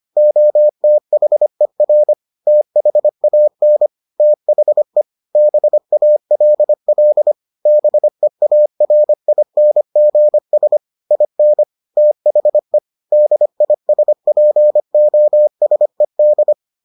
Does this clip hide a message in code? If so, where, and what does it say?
Yes: Morse code from 0.2 to 16.5 s, heard as a keyed tone, about 25 wpm, 600 Hz. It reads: OTHER THAN THE BALL BEARINGS IN THE DISPOSED